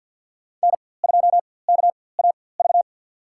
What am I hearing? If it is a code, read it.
Morse code, heard as a keyed tone, about 53 wpm, spaced as for 24 wpm, 700 Hz. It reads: N2KAV
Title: USA Call Signs - 5 Characters - 4x Character Spacing - 50wpm